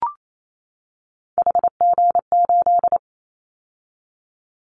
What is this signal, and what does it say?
Morse code, heard as a keyed tone, about 28 wpm, 700 Hz. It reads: HG8
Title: Callsign Prefixes for Top 25 DXCC - 28wpm